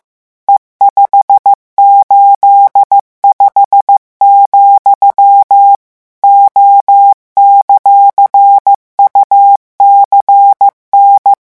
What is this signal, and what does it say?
Morse code, heard as a keyed tone, about 15 wpm, 800 Hz. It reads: E585, O;UCN